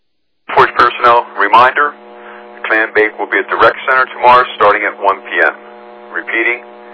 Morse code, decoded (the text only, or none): none